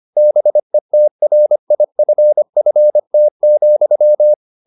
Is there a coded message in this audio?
Yes: Morse code from 0.2 to 4.3 s, heard as a keyed tone, about 25 wpm, 600 Hz. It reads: BETRIFFT,